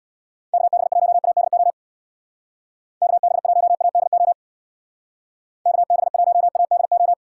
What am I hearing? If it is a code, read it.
Morse code, heard as a keyed tone, about 62 wpm, 700 Hz. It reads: KB1ADO KB1ADO KB1ADO
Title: Speed Racing - USA 2x3 Call Signs 50wpm